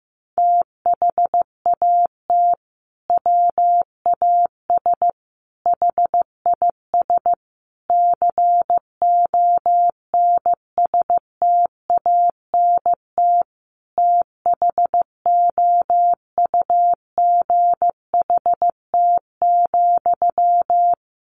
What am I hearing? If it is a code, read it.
Morse code, heard as a keyed tone, about 15 wpm, 700 Hz. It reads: THAT WAS HIS CONSTANT THOUGHT,
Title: Sentences from Top 1500 Words 15wpm